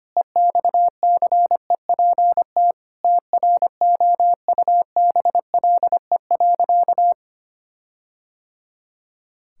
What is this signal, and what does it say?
Morse code, heard as a keyed tone, about 25 wpm, 700 Hz. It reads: EXCEPT TROUBLE.
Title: Sentences from Top 1600 Words 25wpm